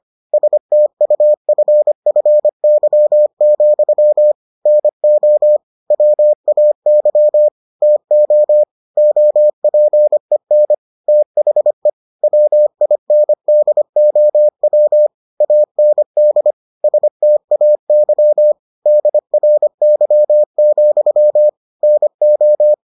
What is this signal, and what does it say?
Morse code, heard as a keyed tone, about 25 wpm, 600 Hz. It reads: STUFFY, NO WAY TO OPEN THE WINDOW AND STAY DRY, NO